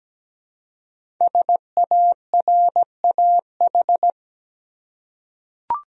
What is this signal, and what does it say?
Morse code, heard as a keyed tone, about 17 wpm, 700 Hz. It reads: SARAH
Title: USA Common Names - 17wpm